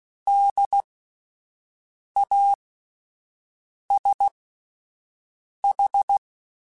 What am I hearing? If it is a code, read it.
Morse code, heard as a keyed tone, about 16 wpm, spaced as for 5 wpm, 800 Hz. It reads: DASH
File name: Spaced Alphas / Spaced Alphas mp3s / Ds sp @10